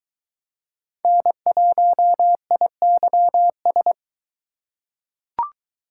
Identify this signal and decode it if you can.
Morse code, heard as a keyed tone, about 20 wpm, 700 Hz. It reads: N1IYS